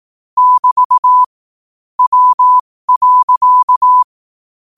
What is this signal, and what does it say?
Morse code, heard as a keyed tone, about 18 wpm, spaced as for 15 wpm, 1000 Hz. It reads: <BT> W.
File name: qotdfiles / 15wpm